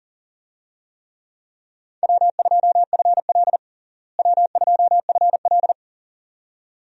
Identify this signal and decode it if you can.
Morse code, heard as a keyed tone, about 40 wpm, 700 Hz. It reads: W2FL W2FL
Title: USA Call Signs - 4 Characters - Repeated 2x - 40wpm